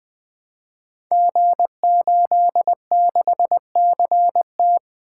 Morse code, heard as a keyed tone, about 20 wpm, 700 Hz. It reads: G86CT
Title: Code Groups - 5 Letter:Numbers - 20wpm